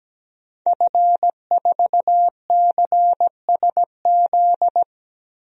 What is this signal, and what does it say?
Morse code, heard as a keyed tone, about 17 wpm, 700 Hz. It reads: F4CSZ
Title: DXCC-100 Call Signs - 5 Characters - 17wpm